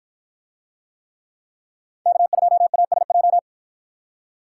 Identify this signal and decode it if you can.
Morse code, heard as a keyed tone, about 49 wpm, 700 Hz. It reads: K2ASJ